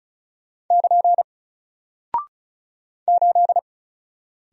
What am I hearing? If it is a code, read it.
Morse code, heard as a keyed tone, about 36 wpm, 700 Hz. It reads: <KN> 8
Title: Single Character (Letter:Number:Punctuation:Prosign) Alt 35wpm